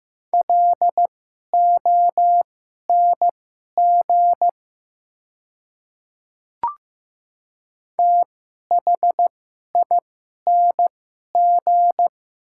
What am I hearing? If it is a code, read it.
Morse code, heard as a keyed tone, about 15 wpm, spaced as for 11 wpm, 690 Hz. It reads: LONG THING
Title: 2x Character Spacing - Top 500 Words - 15wpm